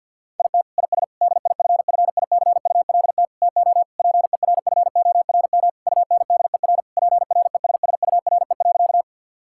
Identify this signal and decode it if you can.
Morse code, heard as a keyed tone, about 50 wpm, 700 Hz. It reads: IT IS DIFFICULT TO PERFORM UNDER PRESSURE.
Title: Sentences from Top 1000 Words - Review Part 6 - 50wpm